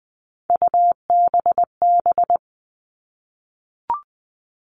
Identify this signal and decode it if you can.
Morse code, heard as a keyed tone, about 20 wpm, 700 Hz. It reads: UBB
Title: All Cut Numbers - 3-Digit 20wpm